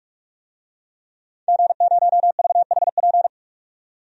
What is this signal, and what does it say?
Morse code, heard as a keyed tone, about 44 wpm, 700 Hz. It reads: G0VHP
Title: DXCC-100 Call Signs - 5 Characters - 45wpm